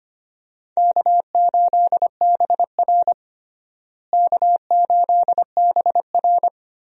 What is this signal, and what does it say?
Morse code, heard as a keyed tone, about 25 wpm, 700 Hz. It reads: K8BR K8BR
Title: USA Call Signs - 4 Characters - Repeated 3x - 25wpm